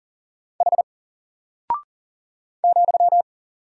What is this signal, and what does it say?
Morse code, heard as a keyed tone, about 38 wpm, 700 Hz. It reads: H ,
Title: Single Character (Letter:Number:Punctuation:Prosign) 40wpm